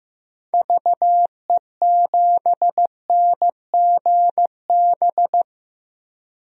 Morse code, heard as a keyed tone, about 15 wpm, 700 Hz. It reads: VE7NGB
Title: DXCC-100 Call Signs - 6 Characters - Repeated 2x - 15wpm